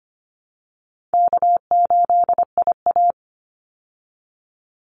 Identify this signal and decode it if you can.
Morse code, heard as a keyed tone, about 25 wpm, 700 Hz. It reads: K8IA
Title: Easy Call Signs Repeated 2x - 25wpm